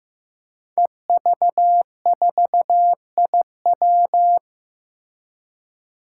Morse code, heard as a keyed tone, about 15 wpm, 700 Hz. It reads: EV4IW
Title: Five Challenging Letter-Number Combinations 15wpm